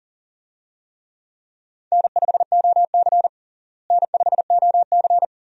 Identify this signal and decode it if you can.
Morse code, heard as a keyed tone, about 37 wpm, 700 Hz. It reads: N5OC N5OC
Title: USA Call Signs - 4 Characters - Repeated 2x - 40wpm